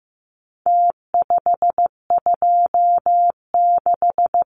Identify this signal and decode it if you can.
Morse code, heard as a keyed tone, about 15 wpm, 700 Hz. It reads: T526
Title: Numbers - Contest Style - 4-Digit 15wpm